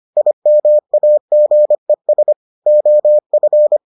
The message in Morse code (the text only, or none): IMAGES OF